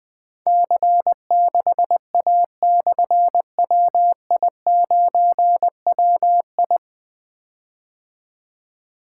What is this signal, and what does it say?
Morse code, heard as a keyed tone, about 20 wpm, 700 Hz. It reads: C6A/WI9WI
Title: Difficult Call Signs 20wpm